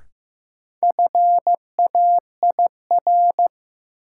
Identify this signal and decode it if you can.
Morse code, heard as a keyed tone, about 16 wpm, 700 Hz. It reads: FAIR
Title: Sentences from Top 1000 Words - Review Part 5 - New Word Drill - 15wpm